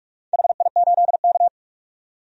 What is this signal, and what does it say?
Morse code, heard as a keyed tone, about 41 wpm, 700 Hz. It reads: HI8K
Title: DXCC-100 Call Signs - 4 Characters - 45wpm